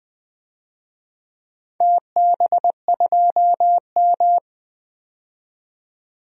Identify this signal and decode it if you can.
Morse code, heard as a keyed tone, about 20 wpm, 700 Hz. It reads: TB2M
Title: DXCC-100 Call Signs - 4 Characters - 20wpm